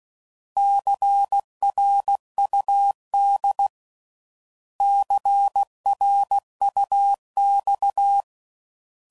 Morse code, heard as a keyed tone, about 16 wpm, 800 Hz. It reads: CRUD CRUX
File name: Prefixes / By First Letter / Cs